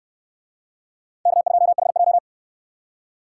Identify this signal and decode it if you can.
Morse code, heard as a keyed tone, about 69 wpm, 700 Hz. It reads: K2HJ